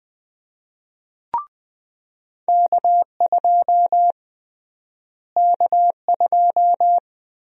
Morse code, heard as a keyed tone, about 20 wpm, 700 Hz. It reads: K2 K2
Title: USA Call Sign Prefixes - Repeated 2x - 20wpm